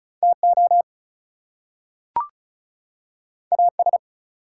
Morse code, heard as a keyed tone, about 36 wpm, 700 Hz. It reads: TO AS
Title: Two Letter Words - 35wpm